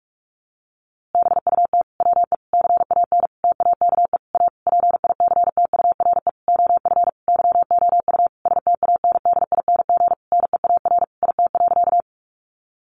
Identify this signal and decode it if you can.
Morse code, heard as a keyed tone, about 44 wpm, 700 Hz. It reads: BUT WE CAN TAKE A PICTURE OF YOU STANDING NEAR IT.
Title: A Sound Of Thunder 45wpm